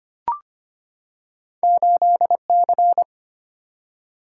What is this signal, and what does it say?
Morse code, heard as a keyed tone, about 25 wpm, 700 Hz. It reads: 8C
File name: Two Letter-Number Combinations - 25wpm